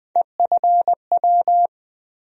Morse code, heard as a keyed tone, about 20 wpm, 700 Hz. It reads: EFW
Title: USA Call Sign Postfixes - 3 Letters 20wpm